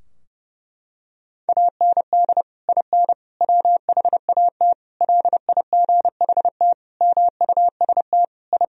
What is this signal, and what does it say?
Morse code, heard as a keyed tone, about 32 wpm, 700 Hz. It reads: AND IN WHAT LIGHT MUST I
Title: Sentences from Top 1000 Words - Review Part 2 - 30wpm